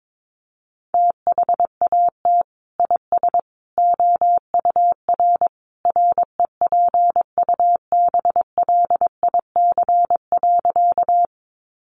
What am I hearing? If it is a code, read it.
Morse code, heard as a keyed tone, about 22 wpm, 700 Hz. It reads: THAT IS OUR REPUBLIC.
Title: Sentences from Top 1500 Words 22wpm